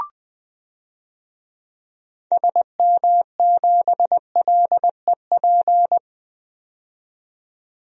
Morse code, heard as a keyed tone, about 20 wpm, 700 Hz. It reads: SM7LEP
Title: DXCC-100 Call Signs - 6 Characters - 20wpm